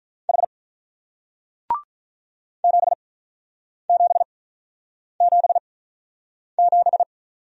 Morse code, heard as a keyed tone, about 45 wpm, spaced as for 9 wpm, 700 Hz. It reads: H 7777